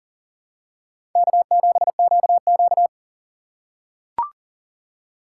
Morse code, heard as a keyed tone, about 40 wpm, 700 Hz. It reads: K7QQ
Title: USA Call Signs - 3x Character Spacing - 40wpm